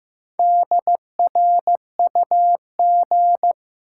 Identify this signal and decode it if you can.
Morse code, heard as a keyed tone, about 15 wpm, 700 Hz. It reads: DRUG
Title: Sets of 5 Words - New Word Drill - 15wpm